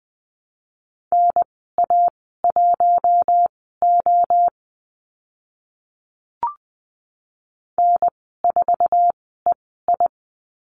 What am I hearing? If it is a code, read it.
Morse code, heard as a keyed tone, about 20 wpm, spaced as for 15 wpm, 700 Hz. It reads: NA1O N4EI
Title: 048 Lesson 9g - Number 4 - Call Signs 20wpm